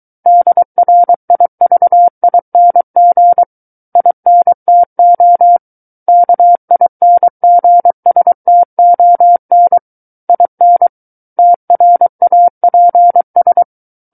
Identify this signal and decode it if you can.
Morse code, heard as a keyed tone, about 23 wpm, 700 Hz. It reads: DRIVING INTO KINGSTON IN TRAPS
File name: Book One / 14-Chapter 14 - 23 WPM